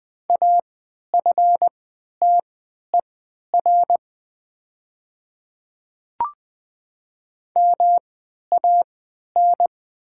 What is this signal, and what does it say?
Morse code, heard as a keyed tone, about 20 wpm, spaced as for 11 wpm, 700 Hz. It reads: AFTER MAN
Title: 3x Character Spacing - Top 300 Words - 20wpm